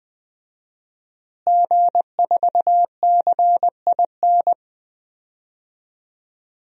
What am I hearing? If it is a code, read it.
Morse code, heard as a keyed tone, about 20 wpm, 700 Hz. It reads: G4CIN
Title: DXCC-100 Call Signs - 5 Characters - Repeated 2x - 20wpm